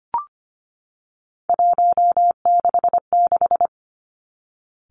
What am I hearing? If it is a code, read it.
Morse code, heard as a keyed tone, about 25 wpm, 700 Hz. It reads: 166
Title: Three Challenging Letter-Number Combinations 25wpm